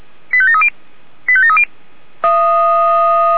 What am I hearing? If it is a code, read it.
Five-tone selective call (ZVEI), heard as quick runs of five tones, tones of about 70 ms. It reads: 85730 85730
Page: SDR/ZVEI